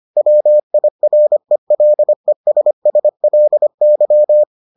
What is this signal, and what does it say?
Morse code, heard as a keyed tone, about 25 wpm, 600 Hz. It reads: WIRELESSLY